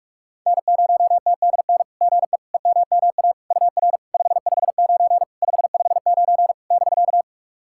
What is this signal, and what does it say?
Morse code, heard as a keyed tone, about 46 wpm, 690 Hz. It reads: N0TDN GE EMMA UR 559 559 <BK>
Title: SOTA QSO Practice 45wpm